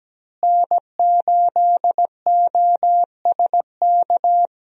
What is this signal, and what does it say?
Morse code, heard as a keyed tone, about 17 wpm, 700 Hz. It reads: N8OSK
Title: USA Call Signs - 5 Characters - Repeated 2x - 17wpm